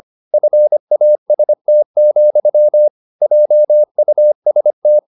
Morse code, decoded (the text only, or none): FAST, JUST